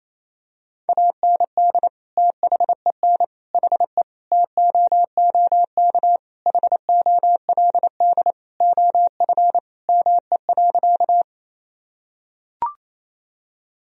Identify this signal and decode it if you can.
Morse code, heard as a keyed tone, about 28 wpm, 700 Hz. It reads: AND THEN HE TOOK HOLD OF ME.
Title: Sentences from Top 1000 Words - Review Part 2 - 28wpm